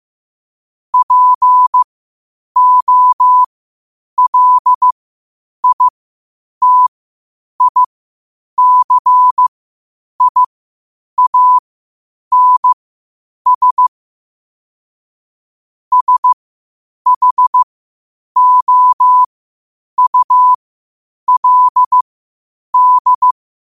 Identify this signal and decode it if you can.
Morse code, heard as a keyed tone, about 15 wpm, spaced as for 8 wpm, 1000 Hz. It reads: POLITICIANS SHOULD
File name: qotdfiles / 8wpm